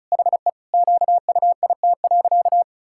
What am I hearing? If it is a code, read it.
Morse code, heard as a keyed tone, about 36 wpm, 700 Hz. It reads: HE QUIT.
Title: Pangram Challenge v2 - 35wpm